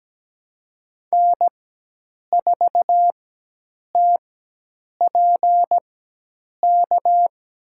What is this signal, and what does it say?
Morse code, heard as a keyed tone, about 17 wpm, spaced as for 8 wpm, 700 Hz. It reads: N4TPK
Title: USA Call Signs - 4x Character Spacing - 17wpm